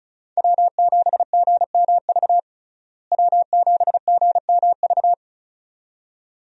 Morse code, heard as a keyed tone, about 36 wpm, 700 Hz. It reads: W7GMV W7GMV
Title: USA Call Signs - 5 Characters - Repeated 2x - 35wpm